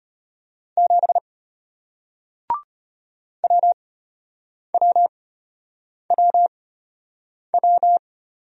Morse code, heard as a keyed tone, about 33 wpm, spaced as for 8 wpm, 700 Hz. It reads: 7 WWWW